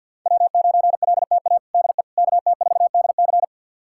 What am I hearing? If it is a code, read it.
Morse code, heard as a keyed tone, about 49 wpm, 700 Hz. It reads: W9LTA DE KT4DC
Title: USA Callsign DE Callsign - 5 Characters - 50wpm